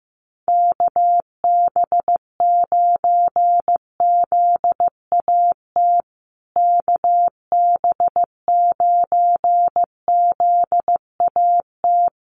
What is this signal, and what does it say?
Morse code, heard as a keyed tone, about 15 wpm, 700 Hz. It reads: KB9ZAT KB9ZAT
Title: USA Call Signs Repeated 3x 15wpm